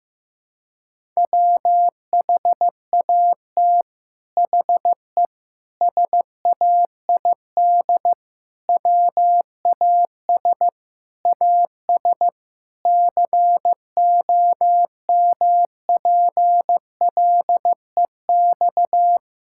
Morse code, heard as a keyed tone, about 16 wpm, 700 Hz. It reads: WHAT HE SAID WAS AS COMPLEX
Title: Sentences from Top 1000 Words - Review Part 6 - 15wpm